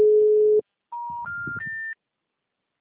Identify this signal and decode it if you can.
Special information tone.
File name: out- -2049- - - .47116